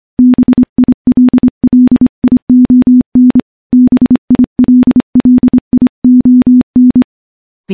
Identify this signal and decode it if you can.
Morse code, heard as a keyed tone, about 25 wpm, 250 Hz. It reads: BILLION BILLION
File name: morse-training-25WPM